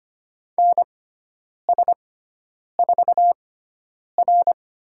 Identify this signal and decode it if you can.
Morse code, heard as a keyed tone, about 25 wpm, spaced as for 9 wpm, 700 Hz. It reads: NS4R
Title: USA Call Signs - 6x Character Spacing - 25wpm